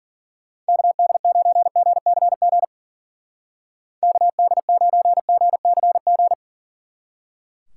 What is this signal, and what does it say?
Morse code, heard as a keyed tone, about 50 wpm, 700 Hz. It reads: KD9GCG KD9GCG
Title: Speed Racing - USA 2x3 Call Signs 40wpm